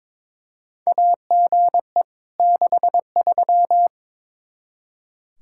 Morse code, heard as a keyed tone, about 22 wpm, 700 Hz. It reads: AGE 63
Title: K1USN SST QSO Practice 22wpm